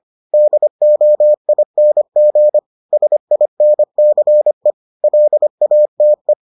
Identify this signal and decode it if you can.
Morse code, heard as a keyed tone, about 25 wpm, 600 Hz. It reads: DOING SINCE LATE